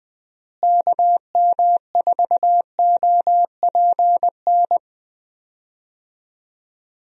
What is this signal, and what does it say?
Morse code, heard as a keyed tone, about 20 wpm, 700 Hz. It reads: KM4OPN
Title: USA Call Signs - 6 Characters - Repeated 2x - 20wpm